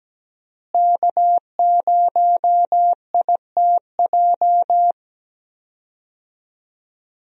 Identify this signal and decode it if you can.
Morse code, heard as a keyed tone, about 17 wpm, 700 Hz. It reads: K0ITJ
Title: USA Call Signs - 5 Characters - Repeated 3x - 17wpm